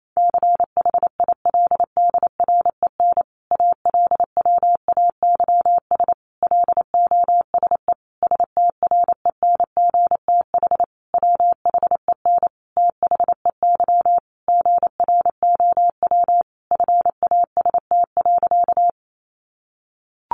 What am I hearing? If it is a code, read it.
Morse code, heard as a keyed tone, about 28 wpm, 700 Hz. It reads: CHILDREN ALWAYS LOSE STRENGTH WHEN THEY GROW FAST.
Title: Sentences from Top 1300 Words 28wpm